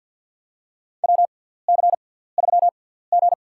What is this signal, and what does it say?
Morse code, heard as a keyed tone, about 52 wpm, spaced as for 17 wpm, 700 Hz. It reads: WC3G
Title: USA Call Signs - 4 Characters - 6x Character Spacing - 50wpm